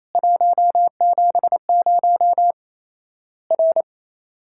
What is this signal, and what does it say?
Morse code, heard as a keyed tone, about 28 wpm, 700 Hz. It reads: 170 R
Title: Worked All Europe DX Contest - QTC Exchange - 28wpm